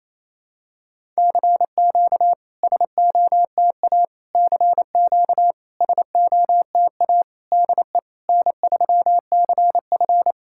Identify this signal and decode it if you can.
Morse code, heard as a keyed tone, about 28 wpm, 700 Hz. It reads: CQ SOTA CQ SOTA DE N3CF
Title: SOTA QSO Practice 28wpm